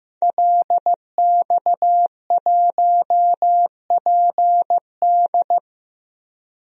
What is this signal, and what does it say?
Morse code, heard as a keyed tone, about 15 wpm, 700 Hz. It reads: LX1PD
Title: DXCC-100 Call Signs - 5 Characters - Repeated 3x - 15wpm